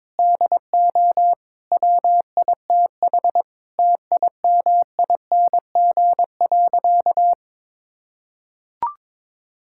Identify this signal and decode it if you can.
Morse code, heard as a keyed tone, about 22 wpm, 700 Hz. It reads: DO WITH TIMING.
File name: practice / 22wpm / Sentences with 8-Word Phrases - 22wpm